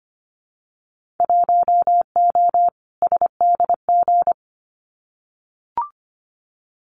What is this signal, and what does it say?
Morse code, heard as a keyed tone, about 25 wpm, 700 Hz. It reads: 1O SDG
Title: Winter Field Day Exchanges Encore 25wpm